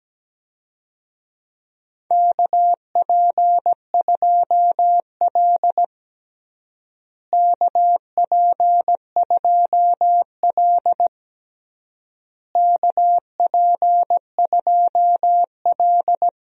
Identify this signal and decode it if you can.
Morse code, heard as a keyed tone, about 17 wpm, 700 Hz. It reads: KP2L KP2L KP2L
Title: DXCC-100 Call Signs - 4 Characters - Repeated 3x - 17wpm